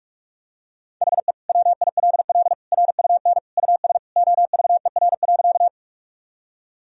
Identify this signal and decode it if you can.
Morse code, heard as a keyed tone, about 47 wpm, 700 Hz. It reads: HE WILL RUN US OVER.
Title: Sentences from Top 1000 Words - Review Part 1 - 45wpm